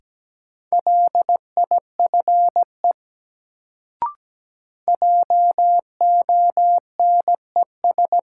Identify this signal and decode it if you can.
Morse code, heard as a keyed tone, about 17 wpm, 700 Hz. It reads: LIFE JONES
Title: Sets of 5 Words - New Word Drill - 17wpm